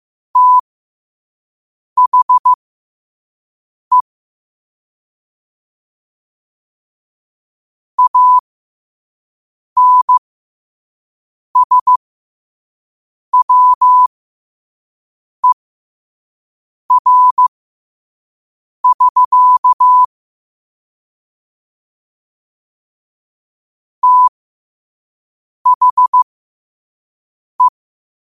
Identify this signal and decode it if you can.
Morse code, heard as a keyed tone, about 15 wpm, spaced as for 5 wpm, 1000 Hz. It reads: THE ANSWER<SK> THE
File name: qotdfiles / 5wpm